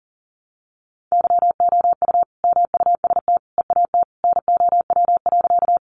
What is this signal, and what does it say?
Morse code, heard as a keyed tone, about 40 wpm, 700 Hz. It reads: YOU MUST EAT NOW.
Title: Sentences from Top 300 Words 40wpm